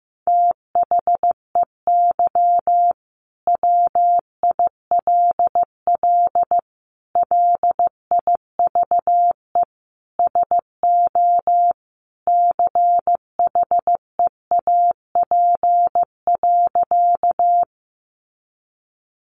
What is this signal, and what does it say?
Morse code, heard as a keyed tone, about 15 wpm, 700 Hz. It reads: THEY WILL LIVE SO CHEAP.
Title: Sentences from Top 1200 Words 15wpm